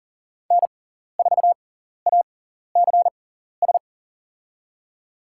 Morse code, heard as a keyed tone, about 40 wpm, spaced as for 14 wpm, 700 Hz. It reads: N4ACS
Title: USA Call Signs - 5 Characters - 6x Character Spacing - 40wpm